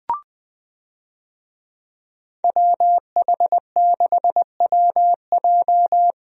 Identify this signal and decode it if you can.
Morse code, heard as a keyed tone, about 20 wpm, 700 Hz. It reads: WH6WJ
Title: USA Call Signs - 5 Characters - 20wpm